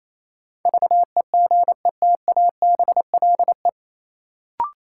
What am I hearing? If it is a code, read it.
Morse code, heard as a keyed tone, about 28 wpm, 700 Hz. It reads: VEGETABLE
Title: Sentences from Top 1300 Words - New Word Drill - 28wpm